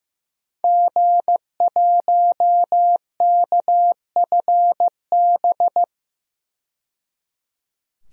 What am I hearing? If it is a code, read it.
Morse code, heard as a keyed tone, about 15 wpm, 700 Hz. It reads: G1KFB
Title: DXCC-100 Call Signs - 5 Characters - 15wpm